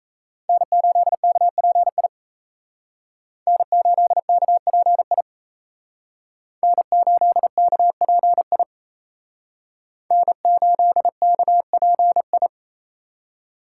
Morse code, heard as a keyed tone, about 35 wpm, 700 Hz. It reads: N8KPI N8KPI N8KPI N8KPI
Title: Speed Racing - USA 1x3 Call Signs 28wpm